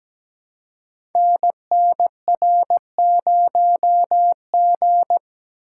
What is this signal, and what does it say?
Morse code, heard as a keyed tone, about 17 wpm, 700 Hz. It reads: NNR0G
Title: Code Groups - 5 Letter:Numbers - 17wpm